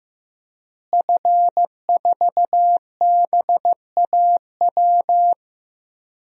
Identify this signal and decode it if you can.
Morse code, heard as a keyed tone, about 15 wpm, 700 Hz. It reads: F4BAW
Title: DXCC-100 Call Signs - 5 Characters - Repeated 3x - 15wpm